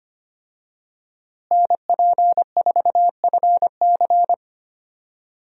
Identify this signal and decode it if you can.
Morse code, heard as a keyed tone, about 25 wpm, 700 Hz. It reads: NP4FC
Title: USA Call Signs - 5 Characters - Repeated 3x - 25wpm